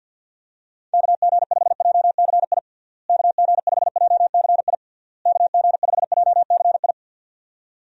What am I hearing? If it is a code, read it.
Morse code, heard as a keyed tone, about 52 wpm, 700 Hz. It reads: KG5JCI KG5JCI KG5JCI
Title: USA Call Signs - 6 Characters - Repeated 3x - 50wpm